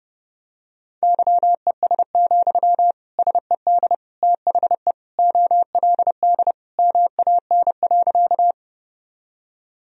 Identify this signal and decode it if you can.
Morse code, heard as a keyed tone, about 32 wpm, 700 Hz. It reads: YES, SED THE OLD MAN.
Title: Sentences from Top 500 Words and Abbreviations 30wpm